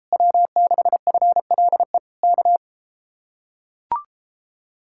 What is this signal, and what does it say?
Morse code, heard as a keyed tone, about 33 wpm, 700 Hz. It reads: W6FLE K